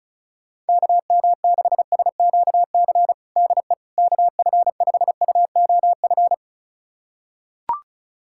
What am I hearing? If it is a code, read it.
Morse code, heard as a keyed tone, about 36 wpm, 700 Hz. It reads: KM6SQC DE KF5UOF
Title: USA Callsign DE Callsign - 6 Characters - 35wpm